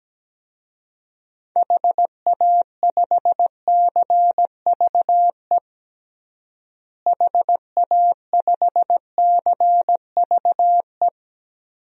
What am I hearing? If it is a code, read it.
Morse code, heard as a keyed tone, about 17 wpm, 700 Hz. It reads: HA5CVE HA5CVE
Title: DXCC-100 Call Signs - 6 Characters - Repeated 2x - 17wpm